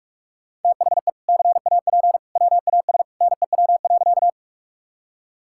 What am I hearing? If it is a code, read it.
Morse code, heard as a keyed tone, about 46 wpm, 700 Hz. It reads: THE CAP WAS NEW.
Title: Sentences from Top 1600 Words 45wpm